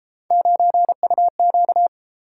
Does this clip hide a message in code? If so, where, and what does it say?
Yes: Morse code from 0.3 to 1.9 s, heard as a keyed tone, about 31 wpm, 700 Hz. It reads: 9UQ